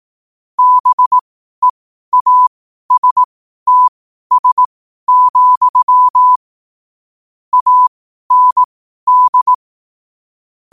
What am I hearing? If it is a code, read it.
Morse code, heard as a keyed tone, about 18 wpm, spaced as for 12 wpm, 1000 Hz. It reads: BEASTS, AND
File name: qotdfiles / 12wpm